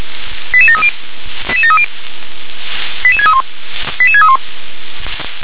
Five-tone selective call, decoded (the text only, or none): 88730 88730 88631 88631